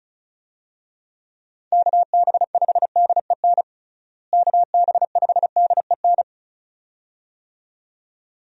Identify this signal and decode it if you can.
Morse code, heard as a keyed tone, about 36 wpm, 700 Hz. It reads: KB5DEN KB5DEN
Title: USA Call Signs - 6 Characters - Repeated 2x - 35wpm